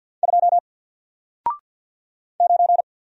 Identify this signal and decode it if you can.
Morse code, heard as a keyed tone, about 48 wpm, 700 Hz. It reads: 2 9
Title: Single Letter-Number Alt 50wpm